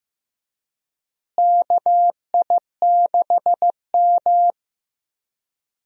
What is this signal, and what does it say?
Morse code, heard as a keyed tone, about 15 wpm, 700 Hz. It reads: KI6M
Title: USA Call Signs - 4 Characters - Repeated 2x - 15wpm